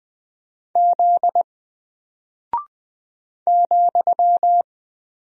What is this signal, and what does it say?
Morse code, heard as a keyed tone, about 20 wpm, 700 Hz. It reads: Z ,
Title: Single Character (Letter:Number:Punctuation:Prosign) Alt 20wpm